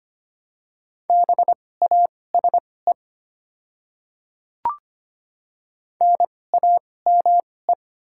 Morse code, heard as a keyed tone, about 25 wpm, spaced as for 18 wpm, 700 Hz. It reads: BASE NAME
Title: 2x Character Spacing - Top 300 Words - 25wpm